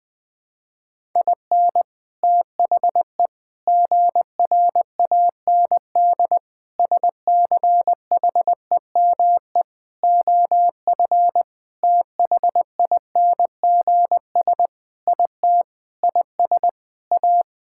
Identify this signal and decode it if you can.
Morse code, heard as a keyed tone, about 20 wpm, 700 Hz. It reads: IN THE GRAND SCHEME OF THINGS IT IS A
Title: Sentences with 6-Word Phrases - 20wpm